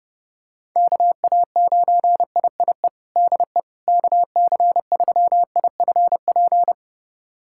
Morse code, heard as a keyed tone, about 32 wpm, 700 Hz. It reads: KA9IIE DE KC3IFP
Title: USA Callsign DE Callsign 30wpm